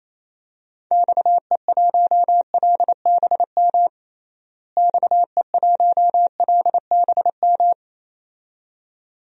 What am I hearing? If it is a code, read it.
Morse code, heard as a keyed tone, about 28 wpm, 700 Hz. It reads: XE1LBM XE1LBM
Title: DXCC-100 Call Signs - 6 Characters - Repeated 2x - 28wpm